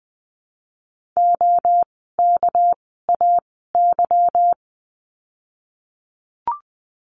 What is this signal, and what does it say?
Morse code, heard as a keyed tone, about 20 wpm, spaced as for 14 wpm, 700 Hz. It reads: OKAY
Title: 199 Lesson 32f - Letter k - Words 20wpm